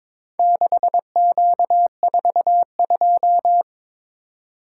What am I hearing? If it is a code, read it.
Morse code, heard as a keyed tone, about 22 wpm, 700 Hz. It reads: 6Q42
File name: practice / 22wpm / Four Letter-Number Combinations - 22wpm